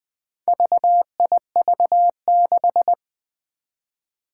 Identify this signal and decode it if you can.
Morse code, heard as a keyed tone, about 20 wpm, 700 Hz. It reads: VIV6
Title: Four Challenging Letter-Number Combinations 20wpm